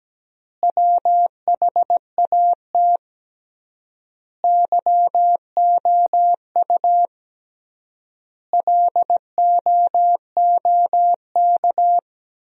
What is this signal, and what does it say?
Morse code, heard as a keyed tone, about 17 wpm, 700 Hz. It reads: WHAT YOU LOOK
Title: 3x Word-Spacing - Sentences from Top 100 Words - 17wpm